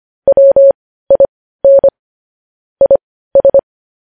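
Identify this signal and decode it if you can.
Morse code, heard as a keyed tone, about 25 wpm, spaced as for 15 wpm, 550 Hz. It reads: WIN IS
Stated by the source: Morse code news headlines sent at 15.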